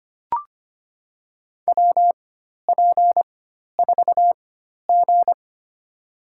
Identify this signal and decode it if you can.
Morse code, heard as a keyed tone, about 25 wpm, spaced as for 12 wpm, 700 Hz. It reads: WP4G
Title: USA Call Signs - 4 Characters - 4x Character Spacing - 25wpm